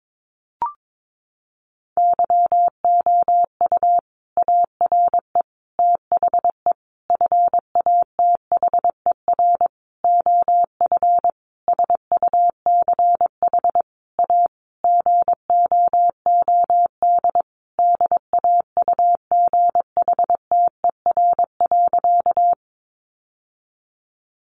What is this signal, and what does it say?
Morse code, heard as a keyed tone, about 22 wpm, 700 Hz. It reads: YOU ARE THE FATHER OF SUCH A GOOD DAUGHTER.
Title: Sentences from Top 1500 Words 22wpm